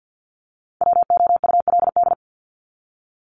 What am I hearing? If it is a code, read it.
Morse code, heard as a keyed tone, about 48 wpm, 700 Hz. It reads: WOULD
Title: Top 200 Words 50wpm